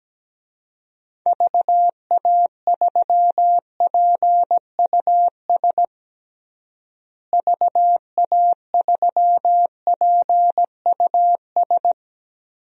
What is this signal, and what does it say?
Morse code, heard as a keyed tone, about 17 wpm, 700 Hz. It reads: VA3PUS VA3PUS
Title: DXCC-100 Call Signs - 6 Characters - Repeated 2x - 17wpm